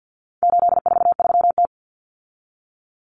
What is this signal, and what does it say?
Morse code, heard as a keyed tone, about 50 wpm, 700 Hz. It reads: 843T
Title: Common Cut Numbers - 4-Digit 50wpm